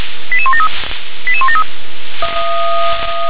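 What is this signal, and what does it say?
Five-tone selective call (ZVEI), heard as quick runs of five tones, tones of about 80 ms. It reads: 88173 88173
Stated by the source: ZVEI five-tone selective call received on an SDR.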